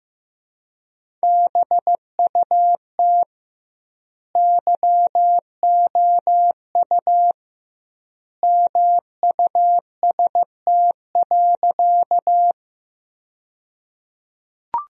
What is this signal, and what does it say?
Morse code, heard as a keyed tone, about 16 wpm, 700 Hz. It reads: BUT YOU MUST.
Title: 2x Word-Spacing - Sentences from Top 200 Words - 15wpm